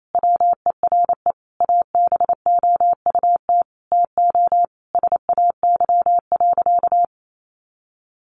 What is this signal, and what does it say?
Morse code, heard as a keyed tone, about 28 wpm, 700 Hz. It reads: WERE ABOUT TO SAY.
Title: Sentences from Top 1200 Words 28wpm